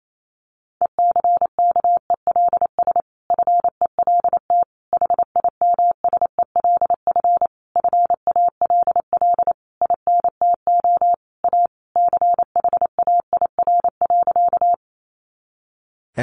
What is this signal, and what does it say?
Morse code, heard as a keyed tone, about 28 wpm, 700 Hz. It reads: ECKELS FELT HIMSELF FALL INTO A CHAIR.
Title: A Sound Of Thunder 28wpm